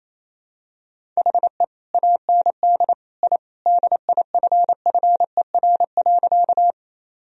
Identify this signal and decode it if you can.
Morse code, heard as a keyed tone, about 28 wpm, 700 Hz. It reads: HE AND I DIFFER.
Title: Sentences from Top 1000 Words - Review Part 1 - 28wpm